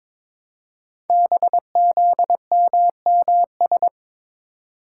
Morse code, heard as a keyed tone, about 22 wpm, 700 Hz. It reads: BZMMS
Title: Code Groups - 5 Letters - 22wpm